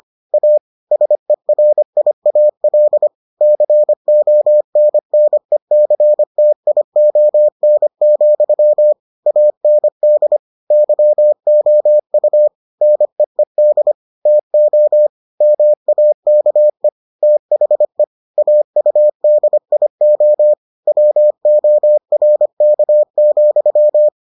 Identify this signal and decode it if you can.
Morse code, heard as a keyed tone, about 25 wpm, 600 Hz. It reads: A SERIAL CONNECTION, AND YOU NEED TO MAKE THE AUDIO WORK,